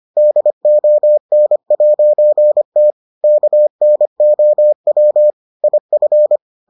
Morse code, heard as a keyed tone, about 25 wpm, 600 Hz. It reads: DON'T KNOW IF